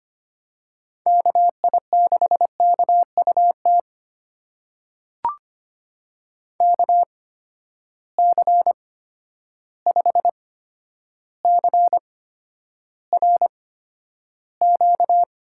Morse code, heard as a keyed tone, about 25 wpm, spaced as for 11 wpm, 700 Hz. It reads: KI6KUT KC5CRQ
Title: USA Call Signs - 6 Characters - 8x Character Spacing - 25wpm